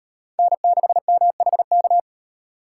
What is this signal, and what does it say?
Morse code, heard as a keyed tone, about 36 wpm, 700 Hz. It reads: N6MHK